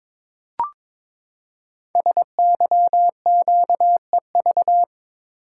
Morse code, heard as a keyed tone, about 22 wpm, 700 Hz. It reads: SYQEV
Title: Code Groups - 5 Letters - 22wpm